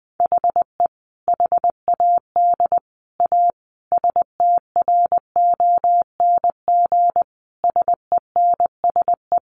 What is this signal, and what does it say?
Morse code, heard as a keyed tone, about 20 wpm, 700 Hz. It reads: HE HAD A STRONG SENSE
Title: Sentences from Top 1500 Words 20wpm